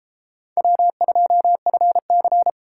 Morse code, heard as a keyed tone, about 32 wpm, 700 Hz. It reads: W2FC